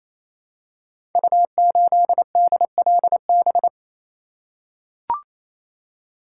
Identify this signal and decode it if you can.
Morse code, heard as a keyed tone, about 27 wpm, 700 Hz. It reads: U8DLB
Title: Code Groups - 5 Letter:Numbers - 28wpm